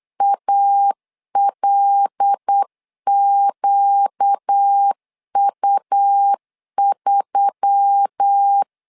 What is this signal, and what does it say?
Morse code, heard as a keyed tone, about 8 wpm, 800 Hz. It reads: ALQU3
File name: LEZIONI / 5 LBVC / 5 L 8wpm 2831